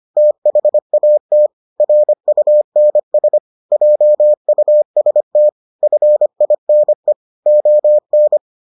Morse code, heard as a keyed tone, about 25 wpm, 600 Hz. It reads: THAT RUNS JUST FINE ON